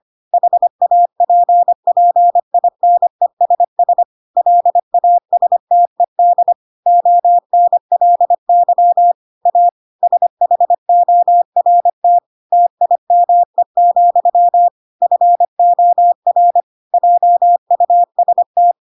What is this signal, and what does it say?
Morse code, heard as a keyed tone, about 25 wpm, 700 Hz. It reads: HAPPINESS LASTED ONLY A SHORT TIME, FOR JUST
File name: Pinocchio / Pinocchio ch4 25 25 ik1hge